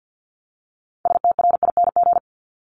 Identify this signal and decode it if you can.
Morse code, heard as a keyed tone, about 50 wpm, 700 Hz. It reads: STRING
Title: Multi-letter Phonemes Repeated 3x 50wpm